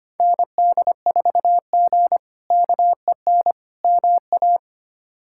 Morse code, heard as a keyed tone, about 25 wpm, 700 Hz. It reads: ND4G KEN MA
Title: K1USN SST QSO Practice - 25wpm